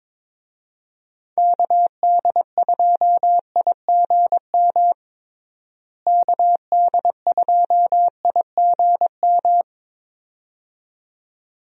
Morse code, heard as a keyed tone, about 22 wpm, 700 Hz. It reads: KD2IGM KD2IGM
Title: USA Call Signs - 6 Characters - Repeated 2x - 22wpm